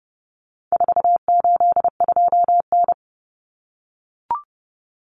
Morse code, heard as a keyed tone, about 30 wpm, 700 Hz. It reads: 482N
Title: Common Cut Numbers - 4-Digit 30wpm